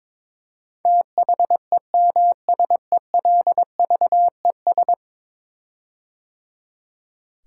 Morse code, heard as a keyed tone, about 22 wpm, 700 Hz. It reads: THEMSELVES
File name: practice / 22wpm / 10 Letter Words - First 3 Letters Given - 22wpm